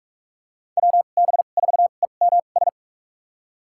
Morse code, heard as a keyed tone, about 46 wpm, spaced as for 33 wpm, 690 Hz. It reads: WB4EMS
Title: USA Call Signs - 6 Characters - 2x Character Spacing - 45wpm